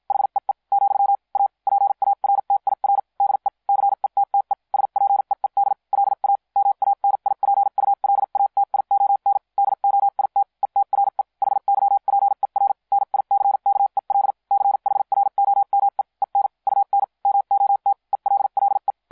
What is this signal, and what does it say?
Morse code, heard as a keyed tone, about 50 wpm, spaced as for 43 wpm, 800 Hz. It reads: HEE , A PARTIR DE CETTE SPEED LA MANIPULATION DOIT ETRE HYPER NICKEL CHROME EN UN MOT ELLE